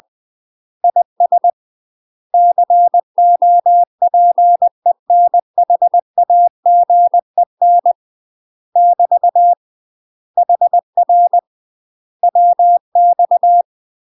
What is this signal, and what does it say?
Morse code, heard as a keyed tone, about 20 wpm, 700 Hz. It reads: IS COPENHAGEN <BT> HR WX
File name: UK MASTER QSOs TRAINING / 20WPM / UK MASTER QSOs 0005